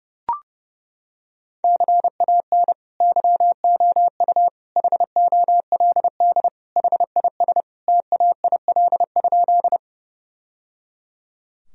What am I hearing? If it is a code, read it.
Morse code, heard as a keyed tone, about 30 wpm, 700 Hz. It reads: CAN YOU HOLD HIS TAIL?
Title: Sentences from Top 1000 Words - Review Part 3 - 30wpm